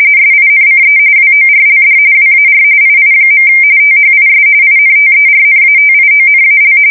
RTTY, about 45.45 baud. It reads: SINCE WORLD WAR II. THIS INTERNATIONAL
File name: fsk170 / battleshipnj